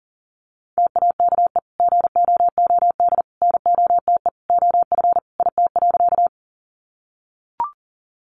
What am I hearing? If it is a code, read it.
Morse code, heard as a keyed tone, about 40 wpm, 700 Hz. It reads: TAKE GOOD NOTE OF IT.
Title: Sentences from Top 500 Words 40wpm